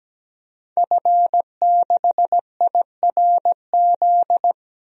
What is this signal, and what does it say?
Morse code, heard as a keyed tone, about 17 wpm, 700 Hz. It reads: F6IRZ
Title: DXCC-100 Call Signs - 5 Characters - Repeated 2x - 17wpm